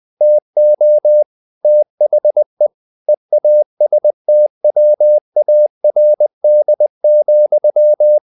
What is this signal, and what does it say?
Morse code, heard as a keyed tone, about 20 wpm, 600 Hz. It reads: TO THE EASTWARD,
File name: the old man and the sea / the old man and the sea part 1 20wpm